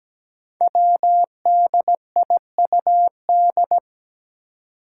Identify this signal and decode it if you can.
Morse code, heard as a keyed tone, about 17 wpm, 700 Hz. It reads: WDIUD
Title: Code Groups - 5 Letter:Numbers - 17wpm